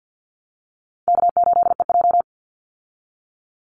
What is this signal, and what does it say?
Morse code, heard as a keyed tone, about 48 wpm, 700 Hz. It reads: K8EJ
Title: Call Signs Repeated 3x - 50wpm